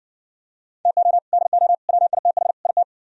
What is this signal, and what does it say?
Morse code, heard as a keyed tone, about 69 wpm, 700 Hz. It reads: TO DO WITH IT